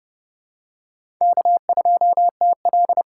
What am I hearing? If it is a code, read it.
Morse code, heard as a keyed tone, about 31 wpm, 700 Hz. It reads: K2TL
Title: USA Call Signs - 5 Characters - 8x Character Spacing - 30wpm